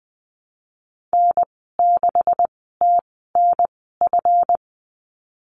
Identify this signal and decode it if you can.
Morse code, heard as a keyed tone, about 20 wpm, spaced as for 14 wpm, 700 Hz. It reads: N6TNF
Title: 245 Lesson 39g - Letter z - Call Signs 20wpm